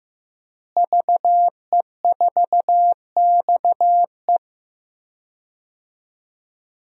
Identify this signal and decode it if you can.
Morse code, heard as a keyed tone, about 15 wpm, 700 Hz. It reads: VE4XE
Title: DXCC-100 Call Signs - 5 Characters - Repeated 2x - 15wpm